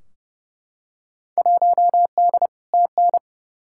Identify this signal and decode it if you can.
Morse code, heard as a keyed tone, about 30 wpm, 700 Hz. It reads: 1D TN
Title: Field Day Exchanges - 30wpm